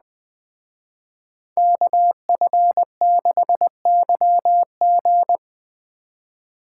Morse code, heard as a keyed tone, about 20 wpm, 700 Hz. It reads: KF6YG
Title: Speed Racing - USA 2x2 Call Signs 17wpm